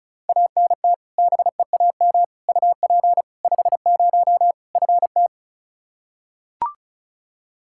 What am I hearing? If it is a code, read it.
Morse code, heard as a keyed tone, about 34 wpm, 700 Hz. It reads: ANT BEAM UP 50 FT
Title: Practice QSO - Exchange 2 - Info - 35wpm